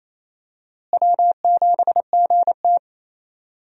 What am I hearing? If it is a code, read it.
Morse code, heard as a keyed tone, about 28 wpm, 700 Hz. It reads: W7GT
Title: USA Call Signs - 4 Characters - Repeated 3x - 28wpm